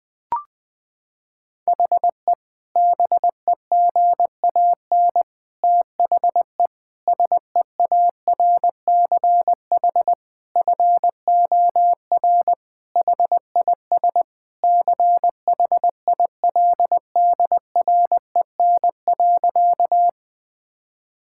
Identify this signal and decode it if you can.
Morse code, heard as a keyed tone, about 20 wpm, 700 Hz. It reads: HE BEGAN THE SEARCH FOR HIS CHILDREN.
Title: Sentences from Top 1000 Words - Review Part 5 - 20wpm